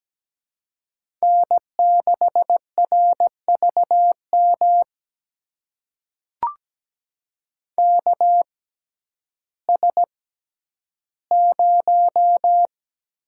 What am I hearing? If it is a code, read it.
Morse code, heard as a keyed tone, about 17 wpm, spaced as for 10 wpm, 700 Hz. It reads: N6RVM KS0
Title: USA Call Signs - 6x Character Spacing - 17wpm